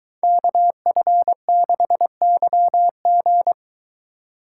Morse code, heard as a keyed tone, about 22 wpm, 700 Hz. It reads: KF6YG